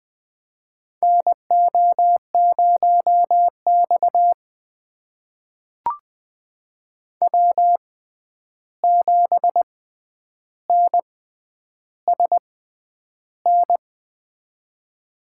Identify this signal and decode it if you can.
Morse code, heard as a keyed tone, about 20 wpm, spaced as for 10 wpm, 700 Hz. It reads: NO0X W7NSN
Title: USA Call Signs - 6x Character Spacing - 20wpm